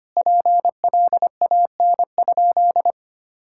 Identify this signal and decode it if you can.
Morse code, heard as a keyed tone, about 25 wpm, 700 Hz. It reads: PLAN?
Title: Sentences from Top 1000 Words - Review Part 6 - 25wpm